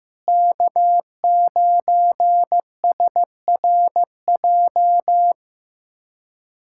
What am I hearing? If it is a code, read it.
Morse code, heard as a keyed tone, about 15 wpm, 700 Hz. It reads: K9SRJ
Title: USA Callsign DE Callsign - 5 Characters - 15wpm